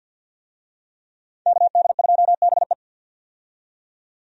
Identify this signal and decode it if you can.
Morse code, heard as a keyed tone, about 48 wpm, 700 Hz. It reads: KD2BE
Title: USA Call Signs - 5 Characters - 50wpm